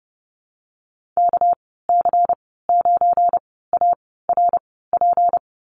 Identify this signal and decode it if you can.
Morse code, heard as a keyed tone, about 30 wpm, spaced as for 17 wpm, 700 Hz. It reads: KC9ARP
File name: Lesson 37g - Letter x - Call Signs 30wpm